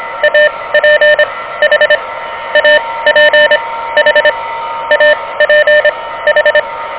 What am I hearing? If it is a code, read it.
Morse code, heard as a keyed tone, about 23 wpm, spaced as for 16 wpm, 590 Hz, with noise. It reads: APH APH APH
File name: OK QRP 2023 / OK QRP 2023